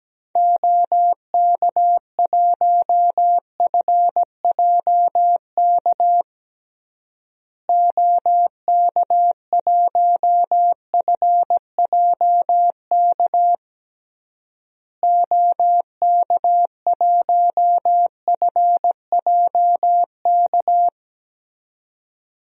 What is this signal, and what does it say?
Morse code, heard as a keyed tone, about 17 wpm, 700 Hz. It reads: OK1FJK OK1FJK OK1FJK
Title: DXCC-100 Call Signs - 6 Characters - Repeated 3x - 17wpm